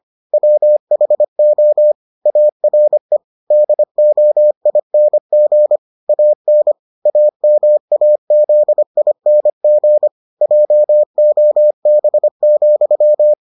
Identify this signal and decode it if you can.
Morse code, heard as a keyed tone, about 25 wpm, 600 Hz. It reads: WHO ARE DOING AN AMAZING JOB,